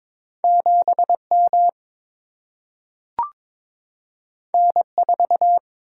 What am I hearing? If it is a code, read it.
Morse code, heard as a keyed tone, about 22 wpm, 700 Hz. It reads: 7 M N 4
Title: Single-Letter Single-Number 22wpm